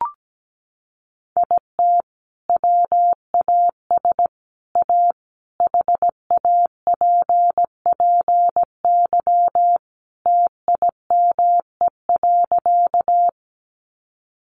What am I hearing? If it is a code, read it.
Morse code, heard as a keyed tone, about 17 wpm, 700 Hz. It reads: IT WAS A HAPPY TIME.
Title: Sentences from Top 1000 Words 17wpm